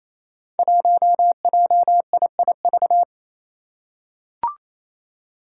Morse code, heard as a keyed tone, about 29 wpm, 700 Hz. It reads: 1JIIV
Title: Five Challenging Letter-Number Combinations 28wpm